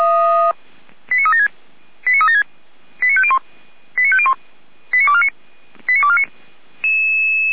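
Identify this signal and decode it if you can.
Five-tone selective call (ZVEI), heard as quick runs of five tones, tones of about 70 ms. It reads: 89376 89376 89501 89501 89249 89249
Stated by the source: ZVEI five-tone selective call received with an SDR.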